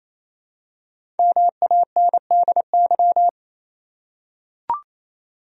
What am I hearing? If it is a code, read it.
Morse code, heard as a keyed tone, about 28 wpm, 700 Hz. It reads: MANDY
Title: USA Common Names - 28wpm